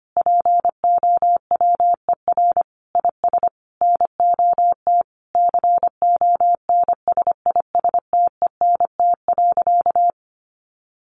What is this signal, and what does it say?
Morse code, heard as a keyed tone, about 25 wpm, 700 Hz. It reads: POWER IS NOT CONSISTENT.
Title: Sentences from Top 1500 Words 25wpm